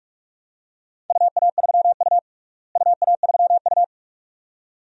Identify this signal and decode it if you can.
Morse code, heard as a keyed tone, about 46 wpm, 700 Hz. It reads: UA3U UA3U
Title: DXCC-100 Call Signs - 4 Characters - Repeated 2x - 45wpm